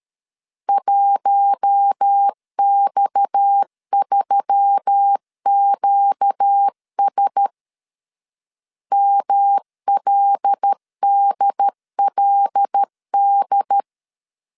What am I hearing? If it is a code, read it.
Morse code, heard as a keyed tone, about 13 wpm, 800 Hz. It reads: 1X3QS MLDLD
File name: CW / DET / 1 6 1500C S A 12wpm 3440